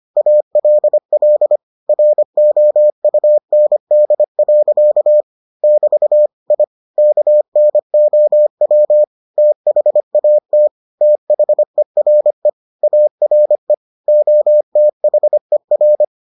Morse code, heard as a keyed tone, about 25 wpm, 600 Hz. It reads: ALL ROUND. <BT> I KNOW THAT THERE ARE OTHER